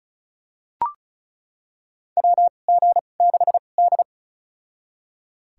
Morse code, heard as a keyed tone, about 34 wpm, spaced as for 25 wpm, 700 Hz. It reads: WG6D
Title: USA Call Signs - 4 Characters - 2x Character Spacing - 35wpm